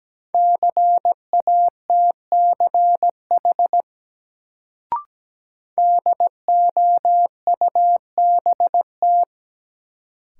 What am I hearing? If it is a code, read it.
Morse code, heard as a keyed tone, about 17 wpm, 700 Hz. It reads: CATCH DOUBT
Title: Sentences from Top 1500 Words - New Word Drill Alt - 17wpm